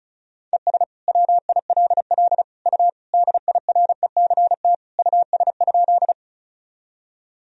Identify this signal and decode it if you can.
Morse code, heard as a keyed tone, about 36 wpm, 700 Hz. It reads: ES WILL U DIRECT US?
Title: Sentences from Top 400 Words and Abbreviations 35wpm